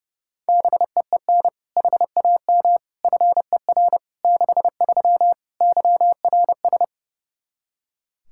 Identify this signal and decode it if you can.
Morse code, heard as a keyed tone, about 32 wpm, 700 Hz. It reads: BEEN HAM FER 63 YRS
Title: Practice QSO - Exchange 3 - More Info - 30wpm